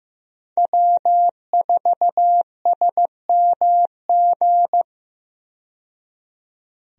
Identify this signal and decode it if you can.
Morse code, heard as a keyed tone, about 15 wpm, 700 Hz. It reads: W4SMG
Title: USA Call Signs - 5 Characters - 15wpm